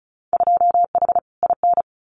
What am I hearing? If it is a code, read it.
Morse code, heard as a keyed tone, about 36 wpm, 700 Hz. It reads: 2H IN
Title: Winter Field Day Exchanges Encore 35wpm